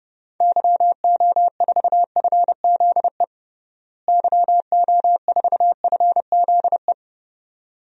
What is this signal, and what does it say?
Morse code, heard as a keyed tone, about 30 wpm, 700 Hz. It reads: YO4FZE YO4FZE
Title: DXCC-100 Call Signs - 6 Characters - Repeated 2x - 30wpm